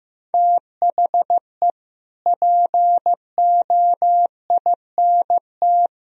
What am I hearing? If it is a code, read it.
Morse code, heard as a keyed tone, about 16 wpm, 700 Hz. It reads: THE POINT
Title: Speed-Racing - Sets of 5 Words 15wpm